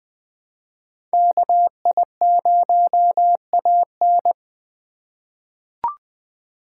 Morse code, heard as a keyed tone, about 20 wpm, 700 Hz. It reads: KI0AN
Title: USA Call Signs - 5 Characters - 4x Character Spacing - 20wpm